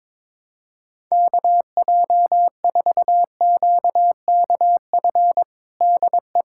Morse code, heard as a keyed tone, about 22 wpm, 700 Hz. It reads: KJ4QKF DE
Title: USA Callsign DE Callsign - 6 Characters - 22wpm